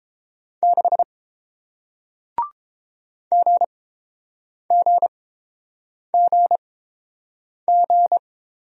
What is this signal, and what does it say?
Morse code, heard as a keyed tone, about 30 wpm, spaced as for 7 wpm, 700 Hz. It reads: 6 GGGG